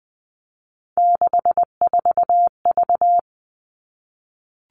Morse code, heard as a keyed tone, about 20 wpm, 700 Hz. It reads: 64V
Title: All Cut Numbers - 3-Digit 20wpm